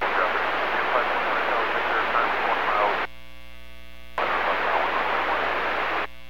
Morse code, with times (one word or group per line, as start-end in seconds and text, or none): none